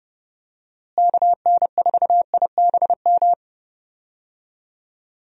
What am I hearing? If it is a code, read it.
Morse code, heard as a keyed tone, about 30 wpm, 700 Hz. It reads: KN4IBM
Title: USA Call Signs - 6 Characters - Repeated 2x - 30wpm